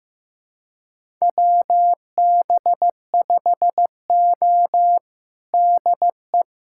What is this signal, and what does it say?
Morse code, heard as a keyed tone, about 16 wpm, 700 Hz. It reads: WB5O DE
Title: USA Callsign DE Callsign - 4 Characters - 15wpm